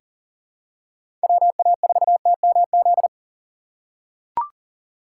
Morse code, heard as a keyed tone, about 37 wpm, 700 Hz. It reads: WA4TMZ
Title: USA Call Signs - 6 Characters - 4x Character Spacing - 40wpm